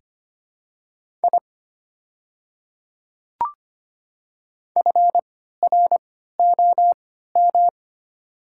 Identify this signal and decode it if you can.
Morse code, heard as a keyed tone, about 25 wpm, spaced as for 14 wpm, 700 Hz. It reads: I FROM
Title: 3x Character Spacing - Top 300 Words - 25wpm